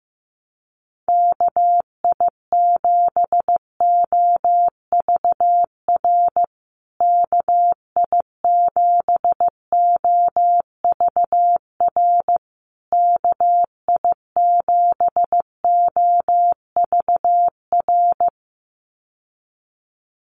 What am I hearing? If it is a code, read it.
Morse code, heard as a keyed tone, about 15 wpm, 700 Hz. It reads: KI7OVR KI7OVR KI7OVR
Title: USA Call Signs Repeated 3x 15wpm